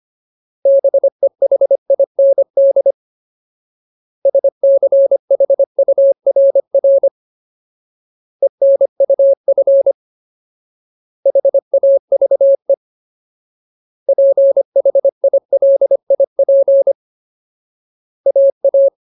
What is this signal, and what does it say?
Morse code, heard as a keyed tone, about 25 wpm, 550 Hz. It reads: BEHIND SCHURR ENUF HAVE PHILIP AA
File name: Ultimate Word List / 25WPM Ultimate